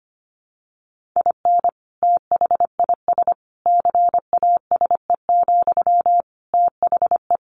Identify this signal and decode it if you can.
Morse code, heard as a keyed tone, about 25 wpm, 700 Hz. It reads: IN THIS CASE, THE
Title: Sentences from Top 1500 Words 25wpm